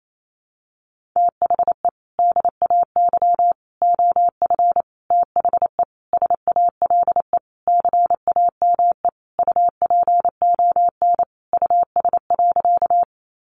Morse code, heard as a keyed tone, about 28 wpm, 700 Hz. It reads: THE DAY OF THE SALE CAME UPON US.
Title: Sentences from Top 1400 Words 28wpm